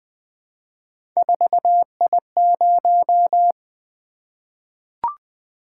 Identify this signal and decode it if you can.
Morse code, heard as a keyed tone, about 20 wpm, 700 Hz. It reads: 4I0
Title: Three Letter-Number Combinations - 20wpm